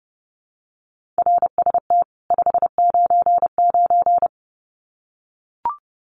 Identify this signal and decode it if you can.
Morse code, heard as a keyed tone, about 29 wpm, 700 Hz. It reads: RST 599
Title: Common QSO Phrases 30wpm